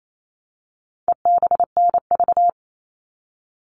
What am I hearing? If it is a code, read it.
Morse code, heard as a keyed tone, about 28 wpm, 700 Hz. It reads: EBNV
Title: All Cut Numbers - 4-Digit 28wpm